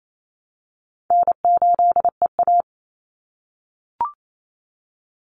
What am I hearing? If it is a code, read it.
Morse code, heard as a keyed tone, about 28 wpm, 700 Hz. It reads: N8EA
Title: Call Signs - 28wpm